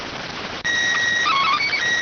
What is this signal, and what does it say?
SSTV (slow-scan television), PD 290.